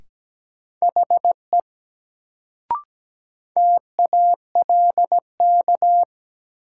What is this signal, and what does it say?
Morse code, heard as a keyed tone, about 17 wpm, 700 Hz. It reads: HE TALK
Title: Sets of 2 Words - New Word Drill - 17wpm